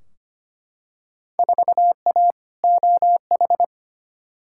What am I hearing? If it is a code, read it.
Morse code, heard as a keyed tone, about 25 wpm, 700 Hz. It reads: 4A OH
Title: Field Day Exchanges - 25wpm